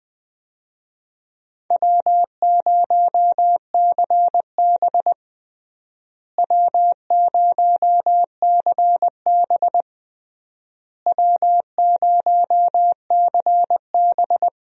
Morse code, heard as a keyed tone, about 20 wpm, 700 Hz. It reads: W0CB W0CB W0CB
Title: USA Call Signs - 4 Characters - Repeated 3x - 20wpm